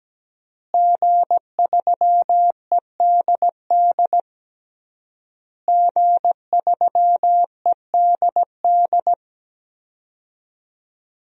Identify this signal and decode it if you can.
Morse code, heard as a keyed tone, about 17 wpm, 700 Hz. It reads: G3EDD G3EDD
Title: DXCC-100 Call Signs - 5 Characters - Repeated 3x - 17wpm